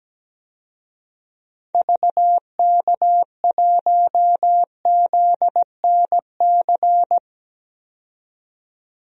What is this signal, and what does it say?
Morse code, heard as a keyed tone, about 17 wpm, 700 Hz. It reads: VK1ZNC
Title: DXCC-100 Call Signs - 6 Characters - 17wpm